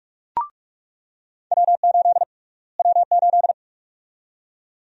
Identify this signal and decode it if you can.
Morse code, heard as a keyed tone, about 42 wpm, 700 Hz. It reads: W8 W8
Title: USA Call Sign Prefixes - Repeated 2x - 45wpm